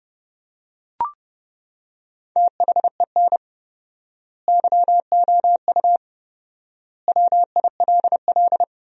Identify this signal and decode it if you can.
Morse code, heard as a keyed tone, about 32 wpm, 700 Hz. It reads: THEN YOU WILL
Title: 4x Word-Spacing - Sentences from Top 400 Words - 30wpm